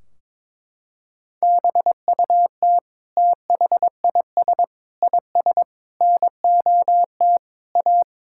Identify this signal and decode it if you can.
Morse code, heard as a keyed tone, about 21 wpm, 700 Hz. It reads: BUT THIS IS NOT A
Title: Sentences from Top 1600 Words 22wpm